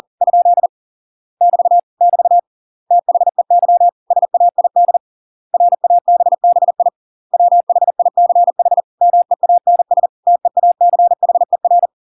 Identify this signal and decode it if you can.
Morse code, heard as a keyed tone, about 40 wpm, 700 Hz. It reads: ? <BT> <BT> THEY SAID RABBI WHICH MEANS TEACHER